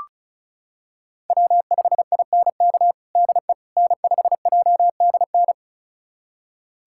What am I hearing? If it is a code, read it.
Morse code, heard as a keyed tone, about 34 wpm, 700 Hz. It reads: W5INK DE N5JDN
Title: USA Callsign DE Callsign - 5 Characters - 35wpm